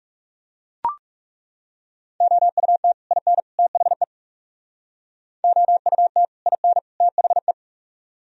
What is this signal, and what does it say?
Morse code, heard as a keyed tone, about 39 wpm, 700 Hz. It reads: OUT IN THE OUT IN THE